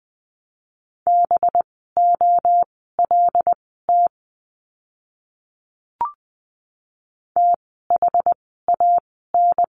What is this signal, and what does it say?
Morse code, heard as a keyed tone, about 20 wpm, spaced as for 15 wpm, 700 Hz. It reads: BOLT THAN
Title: 185 Lesson 30f - Letter b - Words 20wpm